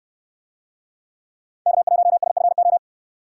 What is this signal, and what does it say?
Morse code, heard as a keyed tone, about 69 wpm, 700 Hz. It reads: K1SFO